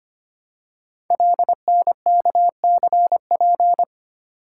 Morse code, heard as a keyed tone, about 25 wpm, 700 Hz. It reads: LNKCP
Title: Code Groups - 5 Letter:Numbers - 25wpm